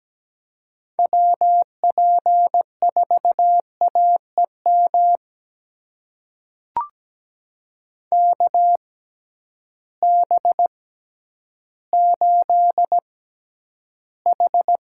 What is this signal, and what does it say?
Morse code, heard as a keyed tone, about 17 wpm, spaced as for 10 wpm, 700 Hz. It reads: WP4AEM KB8H
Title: USA Call Signs - 6 Characters - 6x Character Spacing - 17wpm